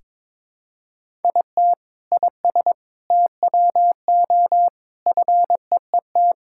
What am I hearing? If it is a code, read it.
Morse code, heard as a keyed tone, about 22 wpm, 700 Hz. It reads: IT IS TWO FEET
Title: Sentences from Top 1000 Words - Review Part 5 - 22wpm